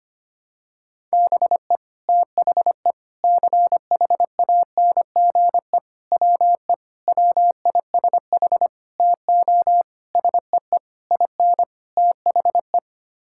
Morse code, heard as a keyed tone, about 25 wpm, 700 Hz. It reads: BE THE CHANGE WE WISH TO SEE IN THE
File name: Sets of 9 Words - 25wpm